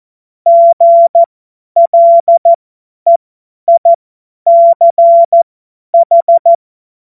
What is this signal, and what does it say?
Morse code, heard as a keyed tone, about 14 wpm, spaced as for 10 wpm, 680 Hz. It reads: GLEICH
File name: Morsekurs / Witze / Witz 15. November 2022 50 F70